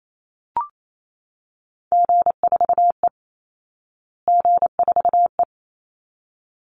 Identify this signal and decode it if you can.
Morse code, heard as a keyed tone, about 28 wpm, 700 Hz. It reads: G4E G4E
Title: Call Signs Repeated 3x - 28wpm